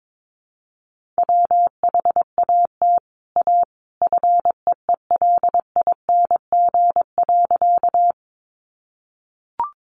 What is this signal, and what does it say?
Morse code, heard as a keyed tone, about 22 wpm, 700 Hz. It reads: WHAT A FEELING.
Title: Sentences from Top 1200 Words 22wpm